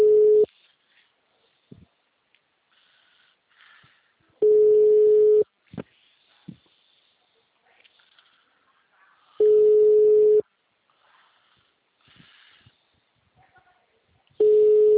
Ringback tone (Europe, 425 Hz).
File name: out- -2007- - - .502